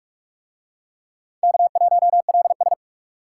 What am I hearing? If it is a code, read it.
Morse code, heard as a keyed tone, about 44 wpm, 700 Hz. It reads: K1LS